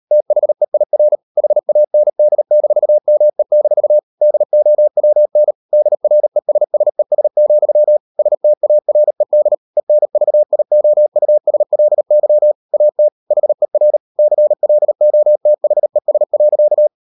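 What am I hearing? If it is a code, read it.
Morse code, heard as a keyed tone, about 38 wpm, 600 Hz. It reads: THEIR HAND-ME-DOWN DRESSES, STARED ENVIOUSLY AT HER CLOTHES.
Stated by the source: Morse code practice at 38 WPM, sending Ken Follett's novel Never in English, from CW med Gnister.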